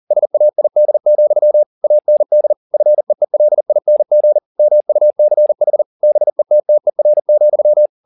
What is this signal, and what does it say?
Morse code, heard as a keyed tone, about 40 wpm, 600 Hz. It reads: SAID, AND FEELING MUCH BETTER,